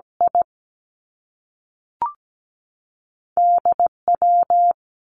Morse code, heard as a keyed tone, about 17 wpm, 700 Hz. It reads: I D W
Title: Two Challenging Letter-Number Combinations 17wpm